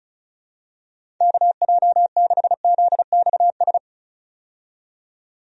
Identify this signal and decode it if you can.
Morse code, heard as a keyed tone, about 34 wpm, 700 Hz. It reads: KJ6ZXS
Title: USA Call Signs - 6 Characters - 35wpm